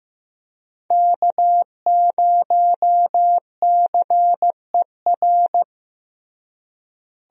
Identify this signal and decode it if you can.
Morse code, heard as a keyed tone, about 15 wpm, 700 Hz. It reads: K0CER
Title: USA Call Signs - 5 Characters - Repeated 2x - 15wpm